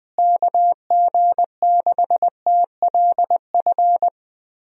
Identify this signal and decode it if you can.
Morse code, heard as a keyed tone, about 21 wpm, 700 Hz. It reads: KG6TLF
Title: USA Call Signs - 6 Characters - Repeated 3x - 20wpm